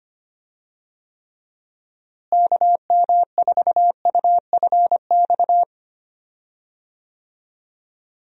Morse code, heard as a keyed tone, about 25 wpm, 700 Hz. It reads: KM4UFX
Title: USA Call Signs - 6 Characters - Repeated 2x - 25wpm